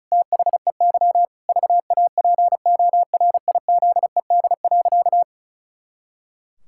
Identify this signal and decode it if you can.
Morse code, heard as a keyed tone, about 36 wpm, 700 Hz. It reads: THEY VAPORIZED.
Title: Pangram Challenge v2 - 35wpm